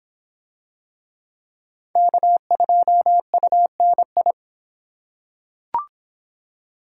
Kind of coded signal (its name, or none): Morse code